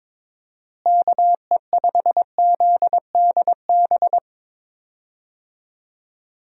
Morse code, heard as a keyed tone, about 22 wpm, 700 Hz. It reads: KE5ZDB
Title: USA Call Signs - 6 Characters - Repeated 2x - 22wpm